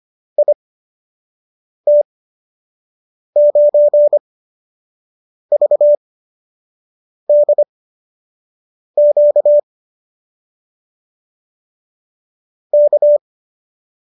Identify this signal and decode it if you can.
Morse code, heard as a keyed tone, about 25 wpm, spaced as for 6 wpm, 590 Hz. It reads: IT9VDQ K